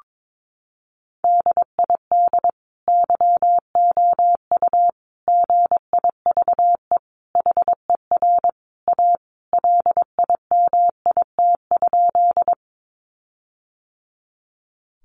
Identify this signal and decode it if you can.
Morse code, heard as a keyed tone, about 22 wpm, 700 Hz. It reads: DID YOU GIVE HER A LIMIT?
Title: Sentences from Top 1500 Words 22wpm